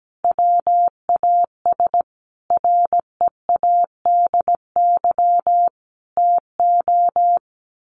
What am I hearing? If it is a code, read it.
Morse code, heard as a keyed tone, about 17 wpm, 700 Hz. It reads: WAS READY TO
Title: Sentences from Top 1400 Words 17wpm